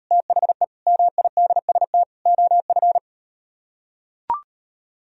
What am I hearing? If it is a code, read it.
Morse code, heard as a keyed tone, about 38 wpm, 700 Hz. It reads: THE MIDST OF